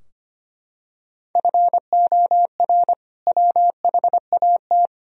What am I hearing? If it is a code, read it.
Morse code, heard as a keyed tone, about 25 wpm, 700 Hz. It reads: FOR WHAT
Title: Sentences from Top 1000 Words - Review Part 4 - 25wpm